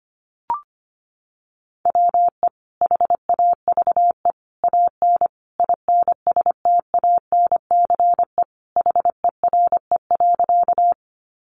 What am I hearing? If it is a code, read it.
Morse code, heard as a keyed tone, about 25 wpm, 700 Hz. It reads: WE HAVE AN INSTANCE HERE.
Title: Sentences from Top 800 Words 25wpm